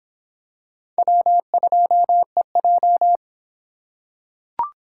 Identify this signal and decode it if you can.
Morse code, heard as a keyed tone, about 26 wpm, 700 Hz. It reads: W2EJ